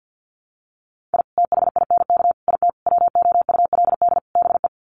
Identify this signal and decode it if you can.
Morse code, heard as a keyed tone, about 50 wpm, 700 Hz. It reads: I THINK IT WOULD BE
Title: Sets of 5 Words 50wpm